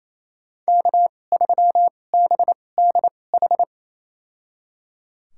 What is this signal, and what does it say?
Morse code, heard as a keyed tone, about 28 wpm, spaced as for 20 wpm, 700 Hz. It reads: K3BDH
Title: USA Call Signs - 5 Characters - 2x Character Spacing - 28wpm